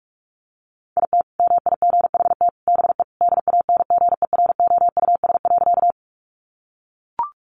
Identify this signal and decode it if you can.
Morse code, heard as a keyed tone, about 44 wpm, 700 Hz. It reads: IT MIGHT BE DANGEROUS.
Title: Sentences from Top 1200 Words 45wpm